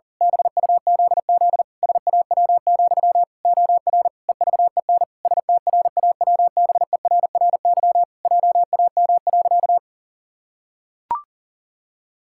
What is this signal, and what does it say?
Morse code, heard as a keyed tone, about 37 wpm, 700 Hz. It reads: BUZZ SAW, OR EVEN STRAWBERRY JAM.
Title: Pangram Challenge v2 - 40wpm